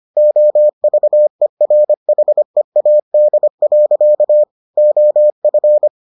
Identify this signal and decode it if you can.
Morse code, heard as a keyed tone, about 25 wpm, 600 Hz. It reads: OVERHEAD. OF